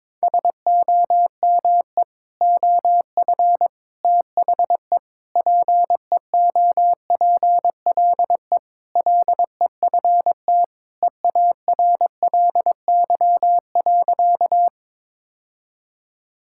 Morse code, heard as a keyed tone, about 22 wpm, 700 Hz. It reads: SOME OF THE PEOPLE LEFT EARLY.
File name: practice / 22wpm / Sentences with 3-Word Phrases - 22wpm